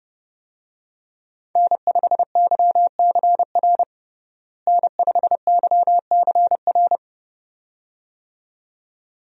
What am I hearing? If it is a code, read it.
Morse code, heard as a keyed tone, about 30 wpm, 700 Hz. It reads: N5YCR N5YCR
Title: USA Call Signs - 5 Characters - Repeated 2x - 30wpm